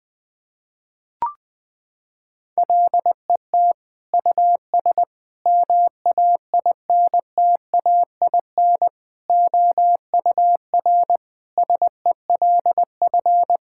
Morse code, heard as a keyed tone, about 20 wpm, 700 Hz. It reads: LET US MAINTAIN OUR SELF
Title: Sentences from Top 1000 Words - Review Part 5 - 20wpm